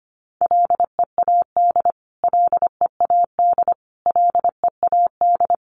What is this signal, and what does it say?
Morse code, heard as a keyed tone, about 25 wpm, 700 Hz. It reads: LEAD LEAD LEAD
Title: Top 500 Words Repeated 25wpm